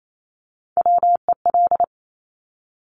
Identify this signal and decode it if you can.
Morse code, heard as a keyed tone, about 28 wpm, 700 Hz. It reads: WEL
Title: Top Three Letter Combinations in Words 28wpm